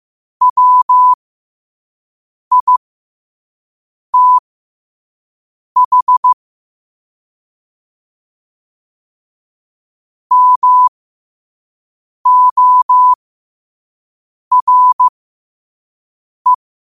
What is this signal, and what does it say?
Morse code, heard as a keyed tone, about 15 wpm, spaced as for 5 wpm, 1000 Hz. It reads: WITH MORE
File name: qotdfiles / 5wpm